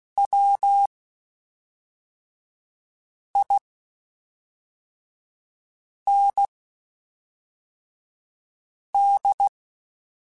Morse code, heard as a keyed tone, about 16 wpm, spaced as for 3 wpm, 800 Hz. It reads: WIND
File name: Spaced Characters / Spaced Characters-mp3s / Weather4 14 sp @5